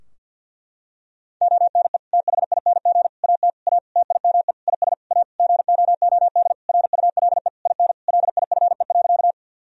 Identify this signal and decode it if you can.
Morse code, heard as a keyed tone, about 48 wpm, 700 Hz. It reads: ONE THING AT A TIME IS A GOOD RULE IN LIFE.
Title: Sentences from Top 1000 Words - Review Part 2 - 50wpm